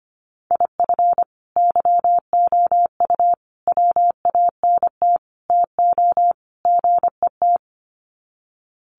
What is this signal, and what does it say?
Morse code, heard as a keyed tone, about 25 wpm, 700 Hz. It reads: IF YOU WANT TO GET
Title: Sets of 5 Words Encore 25wpm